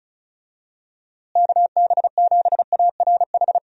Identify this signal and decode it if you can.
Morse code, heard as a keyed tone, about 34 wpm, 700 Hz. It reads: KB7ARH
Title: USA Call Signs - 6 Characters - Repeated 3x - 35wpm